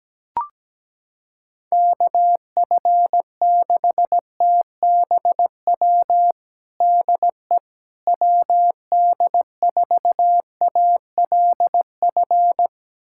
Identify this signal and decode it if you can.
Morse code, heard as a keyed tone, about 17 wpm, 700 Hz. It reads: KF6TBW DE WD4ALF
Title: USA Callsign DE Callsign - 6 Characters - 17wpm